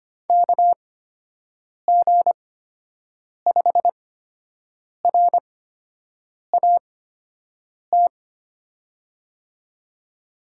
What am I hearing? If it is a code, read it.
Morse code, heard as a keyed tone, about 25 wpm, spaced as for 7 wpm, 700 Hz. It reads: KG5RAT
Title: USA Call Signs - 6 Characters - 8x Character Spacing - 25wpm